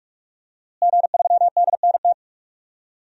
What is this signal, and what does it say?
Morse code, heard as a keyed tone, about 46 wpm, 700 Hz. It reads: G3DNT
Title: DXCC-100 Call Signs - 5 Characters - Repeated 3x - 45wpm